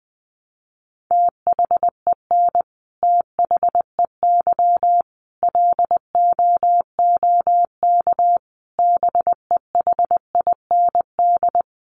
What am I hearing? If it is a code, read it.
Morse code, heard as a keyed tone, about 20 wpm, 700 Hz. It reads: THEN THEY LOOK BEHIND
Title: Sentences from Top 1300 Words 20wpm